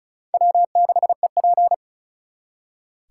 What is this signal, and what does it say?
Morse code, heard as a keyed tone, about 48 wpm, spaced as for 42 wpm, 700 Hz. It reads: W6EP
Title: Speed Racing - USA 1x2 Call Signs 35wpm